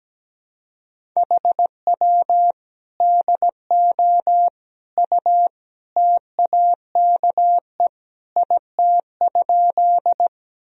Morse code, heard as a keyed tone, about 17 wpm, 700 Hz. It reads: HW DO U TAKE IT?
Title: Sentences from Top 200 Words and Abbreviations 17wpm